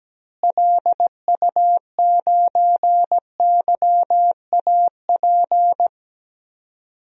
Morse code, heard as a keyed tone, about 17 wpm, 700 Hz. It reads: LU9YAP
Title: DXCC-100 Call Signs - 6 Characters - Repeated 3x - 17wpm